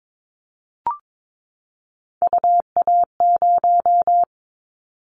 Morse code, heard as a keyed tone, about 22 wpm, 700 Hz. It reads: UA0
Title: Callsign Prefixes for Top 100 DXCC - 22wpm